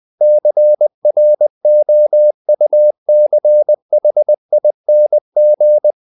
Morse code, heard as a keyed tone, about 20 wpm, 600 Hz. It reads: CROUCHING
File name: the old man and the sea part 2 20wpm